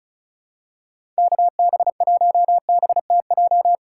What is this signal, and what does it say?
Morse code, heard as a keyed tone, about 36 wpm, 700 Hz. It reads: KB1BTJ
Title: USA Call Signs - 6 Characters - 4x Character Spacing - 35wpm